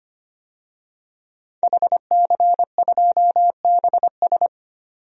Morse code, heard as a keyed tone, about 25 wpm, 690 Hz. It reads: HC2BS
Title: DXCC-100 Call Signs - 5 Characters - 25wpm